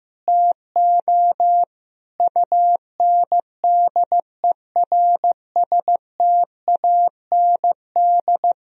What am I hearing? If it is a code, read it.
Morse code, heard as a keyed tone, about 15 wpm, 700 Hz. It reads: TO UNDERSTAND
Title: Sentences from Top 1000 Words - Review Part 2 - 15wpm